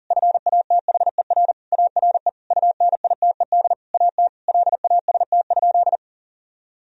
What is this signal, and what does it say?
Morse code, heard as a keyed tone, about 40 wpm, 700 Hz. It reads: FATHER ARE UNITED AT LAST?
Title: Sentences from Top 1000 Words - Review Part 4 - 40wpm